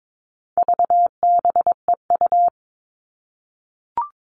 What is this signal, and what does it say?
Morse code, heard as a keyed tone, about 22 wpm, 700 Hz. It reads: VBEU
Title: All Cut Numbers - 4-Digit 22wpm